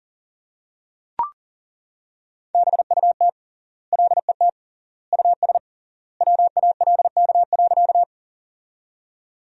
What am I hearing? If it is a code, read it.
Morse code, heard as a keyed tone, about 40 wpm, 700 Hz. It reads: BUT LET US WALK.
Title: 3x Word-Spacing - Sentences from Top 300 Words - 40wpm